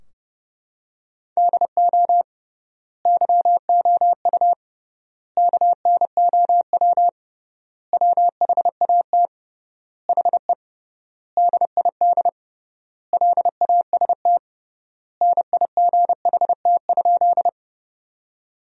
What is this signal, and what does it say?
Morse code, heard as a keyed tone, about 30 wpm, 700 Hz. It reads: DO YOU KNOW WHAT HE DID LAST NIGHT?
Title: 3x Word-Spacing - Sentences from Top 300 Words - 30wpm